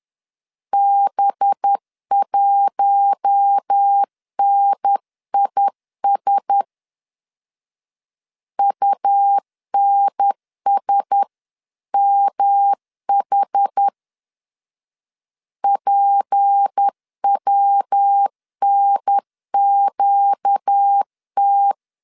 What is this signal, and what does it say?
Morse code, heard as a keyed tone, about 10 wpm, 800 Hz. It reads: B1NIS UNSMH PWNQT
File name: LEZIONI / 6 PKX0 / 6 P 10wpm 2432